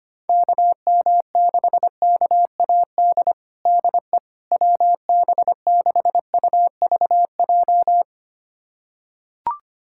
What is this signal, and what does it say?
Morse code, heard as a keyed tone, about 25 wpm, 700 Hz. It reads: KM6KAD DE WB6UVJ
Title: USA Callsign DE Callsign - 6 Characters - 25wpm